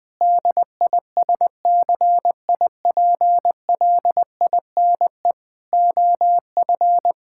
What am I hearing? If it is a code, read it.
Morse code, heard as a keyed tone, about 20 wpm, 700 Hz. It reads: DISCIPLINE OF
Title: Sentences from Top 1000 Words - Review Part 6 - 20wpm